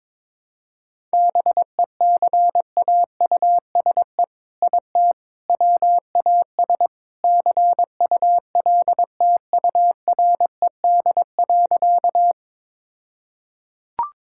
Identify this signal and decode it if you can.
Morse code, heard as a keyed tone, about 22 wpm, 700 Hz. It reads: BECAUSE IT WAS CULTURED.
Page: morse-code-ninja-practice-22wpm